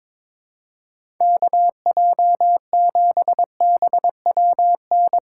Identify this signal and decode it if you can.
Morse code, heard as a keyed tone, about 24 wpm, 700 Hz. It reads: KJ7BWN
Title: Speed Racing - USA 2x3 Call Signs 22wpm